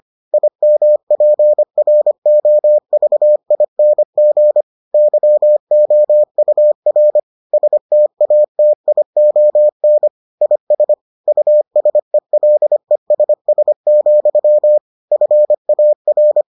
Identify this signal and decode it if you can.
Morse code, heard as a keyed tone, about 25 wpm, 600 Hz. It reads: IMPROVING YOUR STATION IS USELESS, FAR